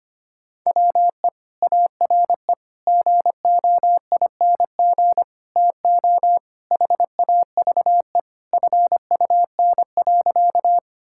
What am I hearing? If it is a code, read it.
Morse code, heard as a keyed tone, about 25 wpm, 700 Hz. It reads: WE ARE GOING TO HAVE FUN.
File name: practice / 25wpm / Sentences with 4-Word Phrases - 25wpm